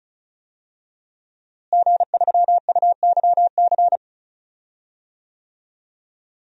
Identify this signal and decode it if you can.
Morse code, heard as a keyed tone, about 36 wpm, 700 Hz. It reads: G3UYC
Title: DXCC-100 Call Signs - 5 Characters - Repeated 3x - 35wpm